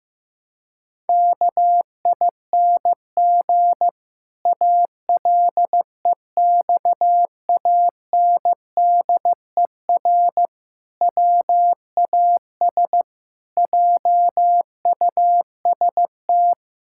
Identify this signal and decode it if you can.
Morse code, heard as a keyed tone, about 15 wpm, 700 Hz. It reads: KING ALEXANDER WAS JUST
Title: Pangram Challenge v2 - 15wpm